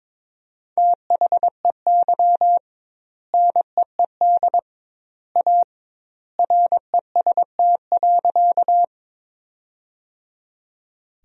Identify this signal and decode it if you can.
Morse code, heard as a keyed tone, about 22 wpm, 700 Hz. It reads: THEY NEED A REST.
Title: 2x Word-Spacing - Sentences from Top 500 Words - 22wpm